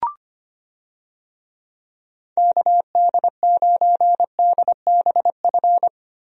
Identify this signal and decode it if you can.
Morse code, heard as a keyed tone, about 25 wpm, 700 Hz. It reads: KD9DBF
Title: USA Call Signs - 6 Characters - Repeated 2x - 25wpm